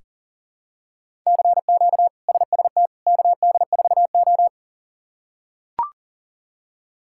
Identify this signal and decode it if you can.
Morse code, heard as a keyed tone, about 40 wpm, 690 Hz. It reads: CQ SST KD4O
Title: K1USN SST QSO Practice - 40wpm